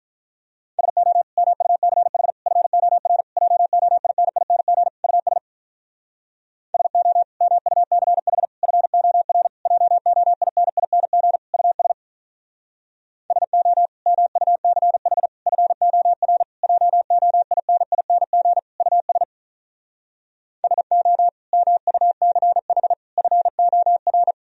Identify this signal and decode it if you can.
Morse code, heard as a keyed tone, about 45 wpm, 700 Hz. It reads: SO MUCH FOR JOINING US SO MUCH FOR JOINING US SO MUCH FOR JOINING US SO MUCH FOR